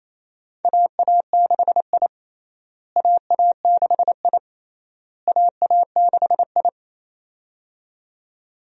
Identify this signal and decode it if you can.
Morse code, heard as a keyed tone, about 28 wpm, 700 Hz. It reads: AA6I AA6I AA6I
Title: USA Call Signs - 4 Characters - Repeated 3x - 28wpm